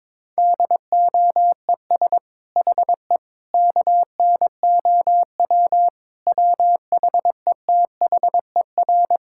Morse code, heard as a keyed tone, about 22 wpm, 700 Hz. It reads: DOES HE KNOW WHETHER
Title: Sentences from Top 1000 Words - Review Part 6 - 22wpm